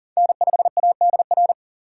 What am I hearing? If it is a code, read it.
Morse code, heard as a keyed tone, about 40 wpm, 700 Hz. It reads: N5ADR
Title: USA Call Signs - 5 Characters - 3x Character Spacing - 40wpm